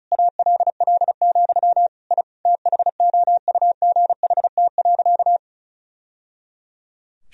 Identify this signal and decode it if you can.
Morse code, heard as a keyed tone, about 36 wpm, 700 Hz. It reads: ALL, I THOUGHT.
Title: Sentences from Top 1000 Words - Review Part 1 - 35wpm